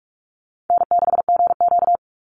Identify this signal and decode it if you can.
Morse code, heard as a keyed tone, about 44 wpm, 700 Hz. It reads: N6GQ
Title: Easy Call Signs - 45wpm